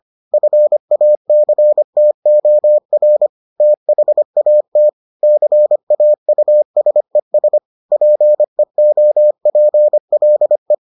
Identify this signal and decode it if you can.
Morse code, heard as a keyed tone, about 25 wpm, 600 Hz. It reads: FACTOR THAT CAUSES PEOPLE